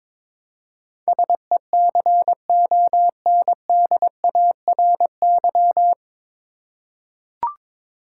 Morse code, heard as a keyed tone, about 22 wpm, 700 Hz. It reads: SECONDARY
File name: 9 Letter Words - First 3 Letters Given - 22wpm